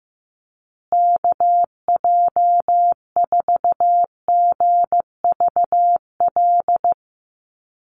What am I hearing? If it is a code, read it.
Morse code, heard as a keyed tone, about 15 wpm, 700 Hz. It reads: KJ4GVL
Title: USA Call Signs Repeated 3x 15wpm